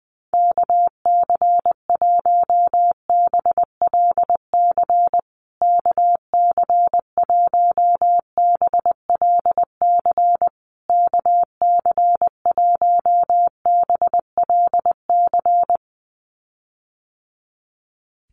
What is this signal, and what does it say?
Morse code, heard as a keyed tone, about 20 wpm, 700 Hz. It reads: KC1BLC KC1BLC KC1BLC
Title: USA Call Signs Repeated 3x 20wpm